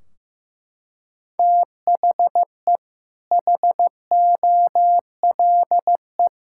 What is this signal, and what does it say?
Morse code, heard as a keyed tone, about 15 wpm, 700 Hz. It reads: THE HOLE
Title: Sentences from Top 1600 Words 15wpm